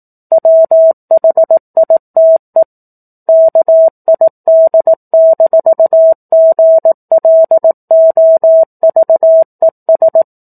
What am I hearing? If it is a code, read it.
Morse code, heard as a keyed tone, about 18 wpm, 650 Hz. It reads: WHITE KID-GLOVES
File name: 04-Chapter-04 18 WPM